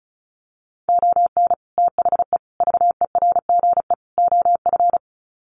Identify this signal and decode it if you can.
Morse code, heard as a keyed tone, about 34 wpm, 700 Hz. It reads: ON THE VERGE OF
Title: Sets of 4 Words 35wpm